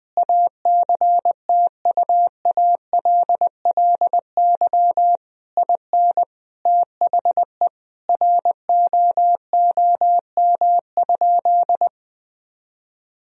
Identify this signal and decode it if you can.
Morse code, heard as a keyed tone, about 21 wpm, 700 Hz. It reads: ACTUALLY IN THE ROOM?
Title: Sentences from Top 1000 Words - Review Part 4 - 20wpm